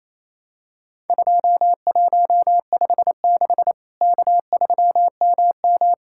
Morse code, heard as a keyed tone, about 28 wpm, 700 Hz. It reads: 2156 K3MM
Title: Worked All Europe DX Contest - QTC Exchange - 28wpm